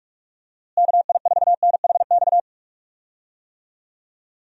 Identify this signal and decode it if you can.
Morse code, heard as a keyed tone, about 44 wpm, 700 Hz. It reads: KI4NHX
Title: USA Call Signs - 6 Characters - Repeated 2x - 45wpm